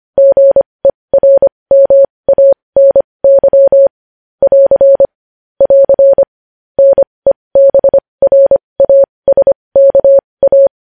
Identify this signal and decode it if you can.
Morse code, heard as a keyed tone, about 25 wpm, spaced as for 20 wpm, 550 Hz. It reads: GERMANY <AR> <AR> NEBRASKA
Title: News Headlines at 20